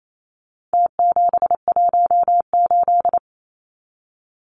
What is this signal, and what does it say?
Morse code, heard as a keyed tone, about 28 wpm, 700 Hz. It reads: T718
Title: Numbers - Contest Style - 4-Digit 28wpm